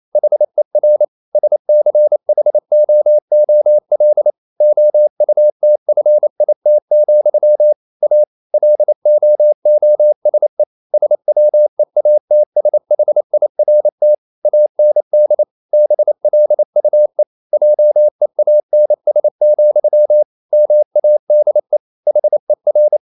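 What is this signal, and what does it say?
Morse code, heard as a keyed tone, about 28 wpm, 600 Hz. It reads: HER SCHOOL OUTFIT, A LOOSE SWEATSHIRT AND BLUE JEANS, MADE HER
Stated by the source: Morse code practice at 28 wpm, text of Ken Follett's novel Never in English, from CW med Gnister.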